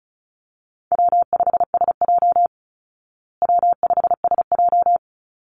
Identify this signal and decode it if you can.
Morse code, heard as a keyed tone, about 34 wpm, 700 Hz. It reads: W5SJ W5SJ
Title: Easy Call Signs Repeated 3x - 35wpm